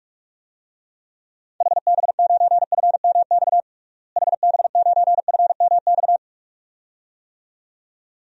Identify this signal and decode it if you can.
Morse code, heard as a keyed tone, about 46 wpm, 700 Hz. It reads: HB9FMX HB9FMX
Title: DXCC-100 Call Signs - 6 Characters - Repeated 2x - 45wpm